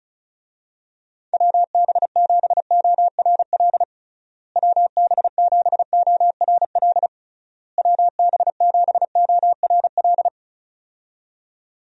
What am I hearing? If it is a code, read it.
Morse code, heard as a keyed tone, about 34 wpm, 700 Hz. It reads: WB7ORL WB7ORL WB7ORL
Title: USA Call Signs - 6 Characters - Repeated 3x - 35wpm